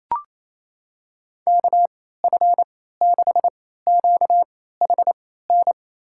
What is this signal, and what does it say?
Morse code, heard as a keyed tone, about 27 wpm, spaced as for 16 wpm, 700 Hz. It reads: KF6QHN
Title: USA Call Signs - 6 Characters - 3x Character Spacing - 28wpm